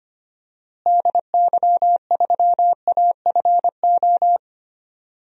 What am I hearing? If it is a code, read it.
Morse code, heard as a keyed tone, about 25 wpm, 700 Hz. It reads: DY3AFO
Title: DXCC-100 Call Signs - 6 Characters - Repeated 2x - 25wpm